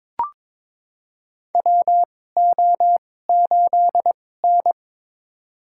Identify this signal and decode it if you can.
Morse code, heard as a keyed tone, about 22 wpm, spaced as for 16 wpm, 700 Hz. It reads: WO8N
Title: USA Call Signs - 4 Characters - 2x Character Spacing - 22wpm